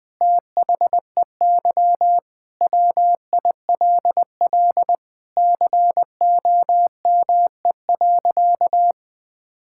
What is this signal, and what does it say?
Morse code, heard as a keyed tone, about 20 wpm, 700 Hz. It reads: THEY WILL COME.
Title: Sentences with 7-Word Phrases - 20wpm